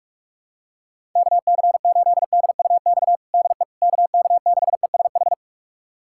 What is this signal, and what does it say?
Morse code, heard as a keyed tone, about 41 wpm, 700 Hz. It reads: KC8DUX DE KK6ESH
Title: USA Callsign DE Callsign 45wpm